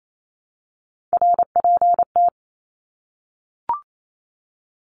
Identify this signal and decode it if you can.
Morse code, heard as a keyed tone, about 28 wpm, 700 Hz. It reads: RPT
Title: Abbreviations 28wpm